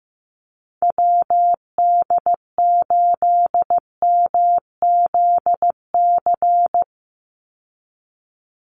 Morse code, heard as a keyed tone, about 15 wpm, 700 Hz. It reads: WD8MZC
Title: USA Call Signs 15wpm